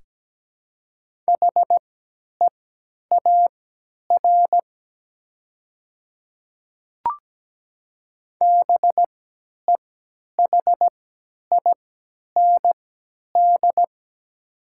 Morse code, heard as a keyed tone, about 17 wpm, spaced as for 10 wpm, 700 Hz. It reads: HEAR BEHIND
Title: 3x Character Spacing - Top 500 Words - 17wpm